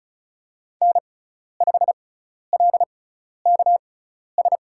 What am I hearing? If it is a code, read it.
Morse code, heard as a keyed tone, about 36 wpm, spaced as for 12 wpm, 700 Hz. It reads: N5LKS
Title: USA Call Signs - 6x Character Spacing - 35wpm